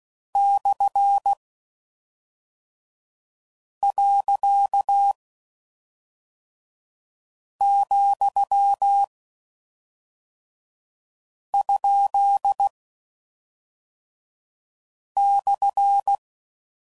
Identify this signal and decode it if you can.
Morse code, heard as a keyed tone, about 16 wpm, 800 Hz. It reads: / . , ? /
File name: Mixed / Misc / Punct